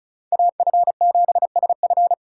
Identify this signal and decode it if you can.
Morse code, heard as a keyed tone, about 43 wpm, 700 Hz. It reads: AF7SF
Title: Speed Racing - USA 2x2 Call Signs 35wpm